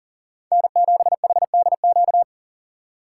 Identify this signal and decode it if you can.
Morse code, heard as a keyed tone, about 37 wpm, 700 Hz. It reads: N7HDQ
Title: USA Call Signs - 2x Character Spacing - 40wpm